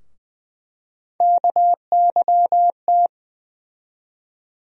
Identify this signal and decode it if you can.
Morse code, heard as a keyed tone, about 20 wpm, 700 Hz. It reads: KYT
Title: USA Call Sign Postfixes - 3 Letters 20wpm